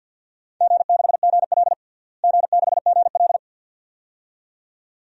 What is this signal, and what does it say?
Morse code, heard as a keyed tone, about 49 wpm, 700 Hz. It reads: G6GL G6GL
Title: DXCC-100 Call Signs - 4 Characters - Repeated 2x - 50wpm